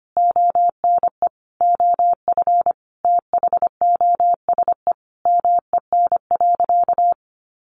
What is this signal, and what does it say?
Morse code, heard as a keyed tone, about 25 wpm, 700 Hz. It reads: ONE OF THOSE MEN.
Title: Sentences from Top 300 Words 25wpm